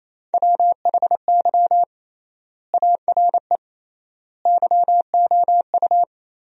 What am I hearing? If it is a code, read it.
Morse code, heard as a keyed tone, about 28 wpm, 700 Hz. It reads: WHY ARE YOU
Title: 3x Word-Spacing - Sentences from Top 200 Words - 28wpm